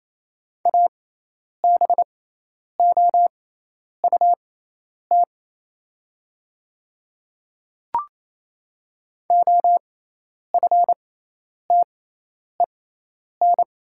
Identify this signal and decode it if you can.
Morse code, heard as a keyed tone, about 28 wpm, spaced as for 10 wpm, 700 Hz. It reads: ABOUT OFTEN
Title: 6x Character Spacing - Top 300 Words - 28wpm